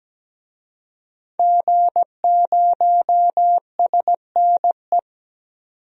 Morse code, heard as a keyed tone, about 17 wpm, 700 Hz. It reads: G0SNE
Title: DXCC-100 Call Signs - 5 Characters - 17wpm